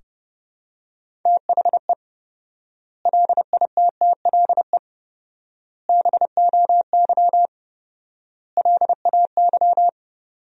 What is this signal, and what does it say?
Morse code, heard as a keyed tone, about 30 wpm, 700 Hz. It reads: THE LITTLE BOY LAY
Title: 4x Word-Spacing - Sentences from Top 500 Words - 30wpm